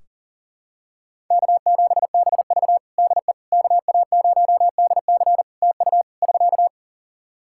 Morse code, heard as a keyed tone, about 37 wpm, 700 Hz. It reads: K7BV DE KA0DC TU <SK>
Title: Practice QSO - Exchange 4 - Ending - 40wpm